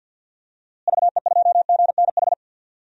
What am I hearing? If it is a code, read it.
Morse code, heard as a keyed tone, about 51 wpm, 700 Hz. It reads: VE2GNH
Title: DXCC-100 Call Signs - 6 Characters - Repeated 3x - 50wpm